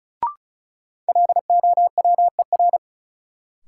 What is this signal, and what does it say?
Morse code, heard as a keyed tone, about 36 wpm, 700 Hz. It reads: LOWER
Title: Sentences from Top 700 Words - New Word Drill - 35wpm